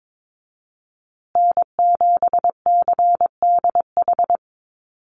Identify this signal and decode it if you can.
Morse code, heard as a keyed tone, about 22 wpm, 700 Hz. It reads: N7CDH
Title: USA Call Signs Repeated 3x 22wpm